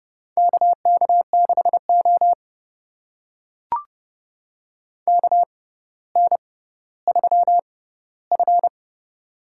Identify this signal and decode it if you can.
Morse code, heard as a keyed tone, about 32 wpm, spaced as for 16 wpm, 700 Hz. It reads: KK6O KN3F
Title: USA Call Signs - 4 Characters - 6x Character Spacing - 30wpm